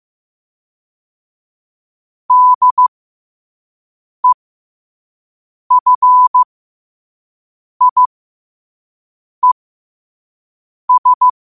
Morse code, heard as a keyed tone, about 15 wpm, spaced as for 5 wpm, 1000 Hz. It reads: DEFIES
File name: qotdfiles / 5wpm